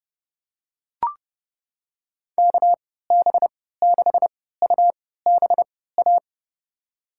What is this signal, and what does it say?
Morse code, heard as a keyed tone, about 29 wpm, spaced as for 17 wpm, 700 Hz. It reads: KB6UBA
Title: USA Call Signs - 6 Characters - 3x Character Spacing - 30wpm